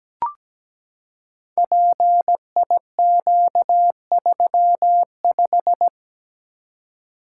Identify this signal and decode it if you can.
Morse code, heard as a keyed tone, about 17 wpm, 700 Hz. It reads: PIQ35
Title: Code Groups - 5 Letter:Numbers - 17wpm